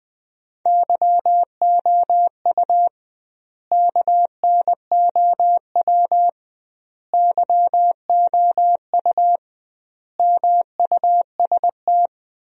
Morse code, heard as a keyed tone, about 20 wpm, 700 Hz. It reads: YOU KNOW YOU MUST
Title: 2x Word-Spacing - Sentences from Top 500 Words - 20wpm